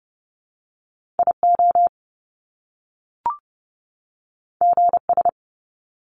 Morse code, heard as a keyed tone, about 30 wpm, 700 Hz. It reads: IO GS
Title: Top Two Letter Combinations in Words 30wpm